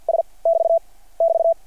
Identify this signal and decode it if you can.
Morse code, heard as a keyed tone, about 51 wpm, 660 Hz. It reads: S <BT> <BT>
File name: archived / CRSnet